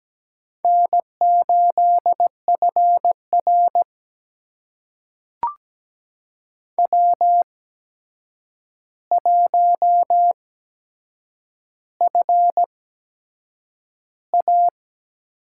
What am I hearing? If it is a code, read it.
Morse code, heard as a keyed tone, about 17 wpm, spaced as for 7 wpm, 700 Hz. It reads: N8FR W1FA
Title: USA Call Signs - 4 Characters - 8x Character Spacing - 17wpm